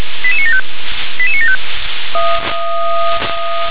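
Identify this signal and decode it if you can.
Five-tone selective call (ZVEI), heard as quick runs of five tones, tones of about 80 ms. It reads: 88975 88975
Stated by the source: ZVEI five-tone selective call received with an SDR.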